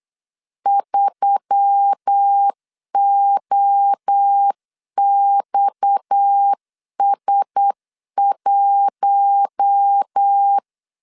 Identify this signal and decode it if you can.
Morse code, heard as a keyed tone, about 8 wpm, 800 Hz. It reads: 3OXS1
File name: CW / DET / 1 6 1500C S C 8wpm 5219